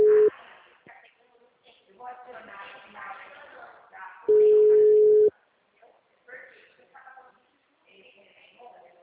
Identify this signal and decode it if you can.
Ringback tone (Europe, 425 Hz), with speech.